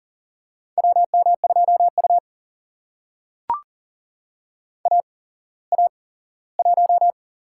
Morse code, heard as a keyed tone, about 42 wpm, spaced as for 20 wpm, 700 Hz. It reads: WM2U AA1
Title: USA Call Signs - 8x Character Spacing - 40wpm